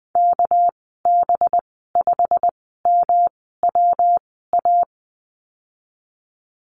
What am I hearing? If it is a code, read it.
Morse code, heard as a keyed tone, about 19 wpm, spaced as for 14 wpm, 700 Hz. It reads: KB5MWA
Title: 207 Lesson 33g - Letter j - Call Signs 20wpm